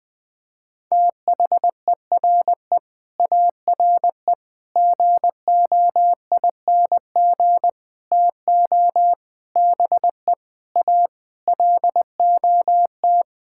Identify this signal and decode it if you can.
Morse code, heard as a keyed tone, about 21 wpm, 700 Hz. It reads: THERE ARE GOING TO BE A LOT
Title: Sentences with 8-Word Phrases - 20wpm